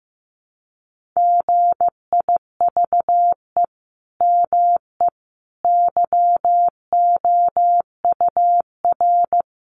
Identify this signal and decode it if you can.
Morse code, heard as a keyed tone, about 15 wpm, 700 Hz. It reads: GIVE ME YOUR
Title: Sentences from Top 1500 Words 15wpm